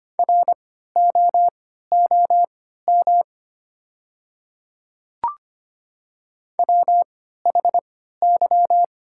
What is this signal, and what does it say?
Morse code, heard as a keyed tone, about 25 wpm, spaced as for 14 wpm, 700 Hz. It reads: ROOM WHY
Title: 3x Character Spacing - Top 300 Words - 25wpm